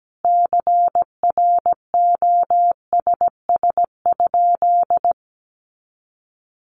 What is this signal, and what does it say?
Morse code, heard as a keyed tone, about 17 wpm, 700 Hz. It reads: CROSS?
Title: Sentences from Top 300 Words 17wpm